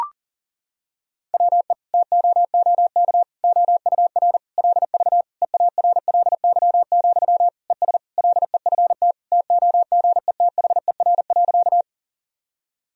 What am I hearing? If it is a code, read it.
Morse code, heard as a keyed tone, about 37 wpm, 700 Hz. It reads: WE TOOK OUR LV EARLY, ES LEFT TOGETHER.
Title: Sentences from Top 400 Words and Abbreviations 40wpm